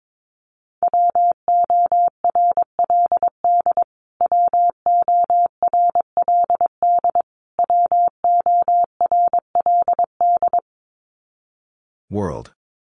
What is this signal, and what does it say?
Morse code, heard as a keyed tone, about 22 wpm, 700 Hz. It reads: WORLD WORLD WORLD
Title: Top 200 Words Repeated 22wpm